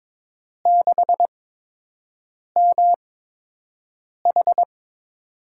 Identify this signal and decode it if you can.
Morse code, heard as a keyed tone, about 22 wpm, spaced as for 6 wpm, 700 Hz. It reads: 6MH
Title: USA Call Signs - 4 Characters - 8x Character Spacing - 22wpm